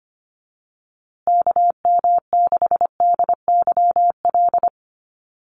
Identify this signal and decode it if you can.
Morse code, heard as a keyed tone, about 25 wpm, 700 Hz. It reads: KM6DYL
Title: USA Call Signs Repeated 3x 25wpm